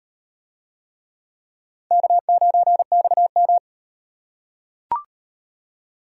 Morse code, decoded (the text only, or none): K9XM